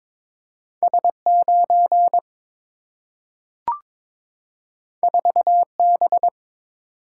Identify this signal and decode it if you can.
Morse code, heard as a keyed tone, about 22 wpm, 700 Hz. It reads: S 9 4 B
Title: Single-Letter Single-Number 22wpm